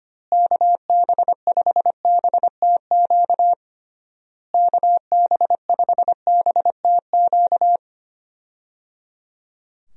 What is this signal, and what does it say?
Morse code, heard as a keyed tone, about 25 wpm, 700 Hz. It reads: KB5BTQ KB5BTQ
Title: USA Call Signs - 6 Characters - Repeated 3x - 25wpm